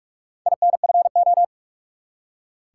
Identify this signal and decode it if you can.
Morse code, heard as a keyed tone, about 44 wpm, 700 Hz. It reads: INFO
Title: Top 180 QSO Elements 45wpm